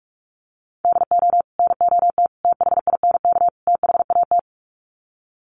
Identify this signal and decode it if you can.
Morse code, heard as a keyed tone, about 44 wpm, 700 Hz. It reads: DO NOT THINK THAT
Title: Sets of 4 Words 45wpm